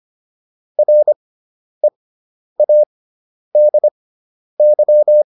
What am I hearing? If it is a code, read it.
Morse code, heard as a keyed tone, about 25 wpm, spaced as for 10 wpm, 600 Hz. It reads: READY